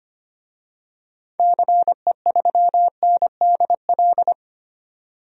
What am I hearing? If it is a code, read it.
Morse code, heard as a keyed tone, about 25 wpm, 700 Hz. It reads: CE3NDL
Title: DXCC-100 Call Signs - 6 Characters - 25wpm